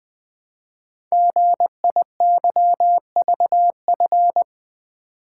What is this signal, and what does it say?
Morse code, heard as a keyed tone, about 20 wpm, 700 Hz. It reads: GIYVF
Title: Code Groups - 5 Letters - 20wpm